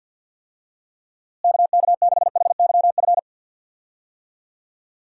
Morse code, heard as a keyed tone, about 53 wpm, 700 Hz. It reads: KK6HYF
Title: USA Call Signs - 6 Characters - Repeated 3x - 50wpm